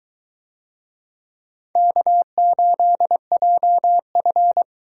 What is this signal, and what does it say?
Morse code, heard as a keyed tone, about 21 wpm, 700 Hz. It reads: K8JF